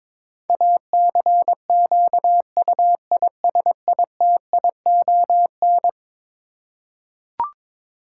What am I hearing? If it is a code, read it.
Morse code, heard as a keyed tone, about 22 wpm, 700 Hz. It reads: ACQUISITION
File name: practice / 22wpm / Eleven Letter Words - 22wpm